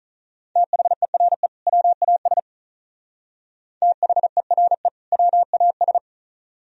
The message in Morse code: THERE WAS THERE WAS

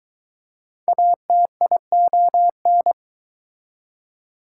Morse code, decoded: ATION

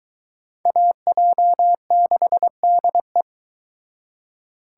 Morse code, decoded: AJ6DE

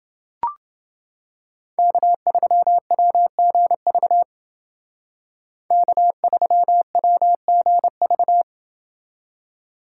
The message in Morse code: K3WGV K3WGV

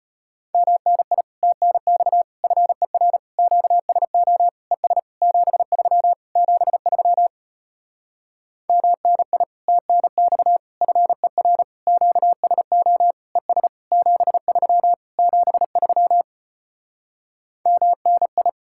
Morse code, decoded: MNI TNX FER QSO ES 73 73 MNI TNX FER QSO ES 73 73 MNI

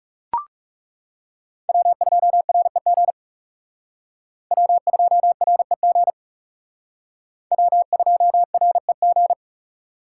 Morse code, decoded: W2REG W2REG W2REG

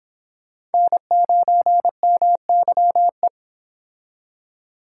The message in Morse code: N9MYE

